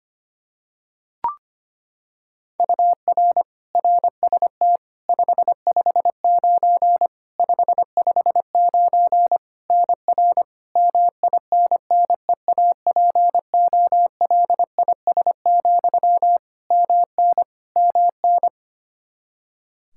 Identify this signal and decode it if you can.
Morse code, heard as a keyed tone, about 25 wpm, 700 Hz. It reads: UR RST 559 559 NR MINNEAPOLIS, MN MN
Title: Practice QSO - Exchange 1 - Beginning - 25wpm